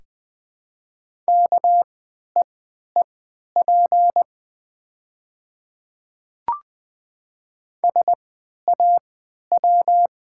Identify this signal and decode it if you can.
Morse code, heard as a keyed tone, about 20 wpm, spaced as for 11 wpm, 700 Hz. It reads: KEEP SAW
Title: 3x Character Spacing - Top 500 Words - 20wpm